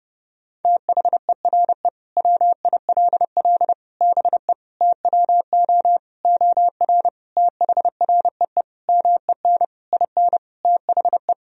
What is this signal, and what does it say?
Morse code, heard as a keyed tone, about 30 wpm, 700 Hz. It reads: THERE WILL BE TWO OR THREE MEN IN THE
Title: Sentences from Top 1000 Words - Review Part 3 - 30wpm